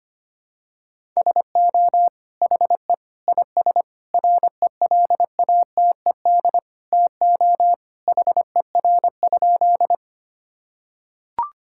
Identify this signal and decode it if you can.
Morse code, heard as a keyed tone, about 25 wpm, 700 Hz. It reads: SO HE IS RELATED TO HER?
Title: Sentences from Top 1000 Words - Review Part 5 - 25wpm